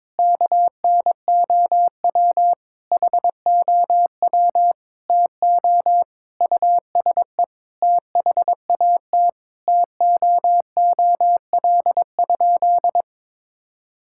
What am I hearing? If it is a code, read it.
Morse code, heard as a keyed tone, about 22 wpm, 700 Hz. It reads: KNOW HOW TO USE THAT TOOL?
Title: Sentences from Top 1000 Words - Review Part 6 - 22wpm